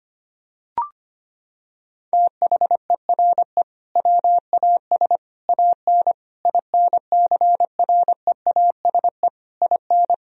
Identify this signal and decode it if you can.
Morse code, heard as a keyed tone, about 25 wpm, 700 Hz. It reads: THERE WAS AN INCREASE IN
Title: Sentences from Top 1000 Words - Review Part 4 - 25wpm